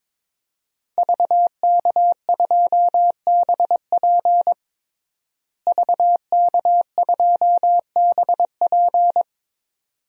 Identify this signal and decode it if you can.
Morse code, heard as a keyed tone, about 22 wpm, 700 Hz. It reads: VK2BP VK2BP
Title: DXCC-100 Call Signs - 5 Characters - Repeated 2x - 22wpm